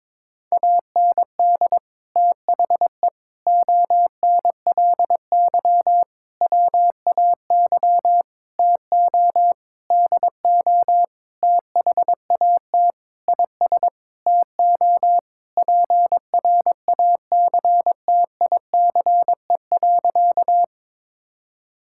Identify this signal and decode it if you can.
Morse code, heard as a keyed tone, about 22 wpm, 700 Hz. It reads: AND THE ONLY WAY TO DO THAT IS TO PRACTICE.
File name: Sentences with 9-Word Phrases - 22wpm